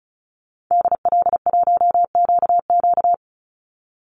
Morse code, heard as a keyed tone, about 34 wpm, 700 Hz. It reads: DL1QQ
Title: Call Signs - 35wpm